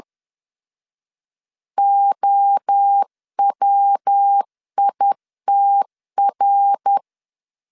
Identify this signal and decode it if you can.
Morse code, heard as a keyed tone, about 10 wpm, 800 Hz. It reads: OWITR
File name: LEZIONI / 7 GF9Z / 7 G 10wpm 2501